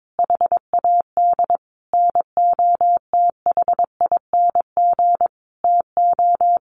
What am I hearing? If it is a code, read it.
Morse code, heard as a keyed tone, about 22 wpm, 700 Hz. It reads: HAD NOTHING TO
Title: Sentences from Top 1500 Words 22wpm